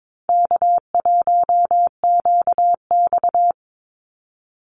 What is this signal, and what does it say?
Morse code, heard as a keyed tone, about 22 wpm, 700 Hz. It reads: K1QX
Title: Easy Call Signs Repeated 3x - 22wpm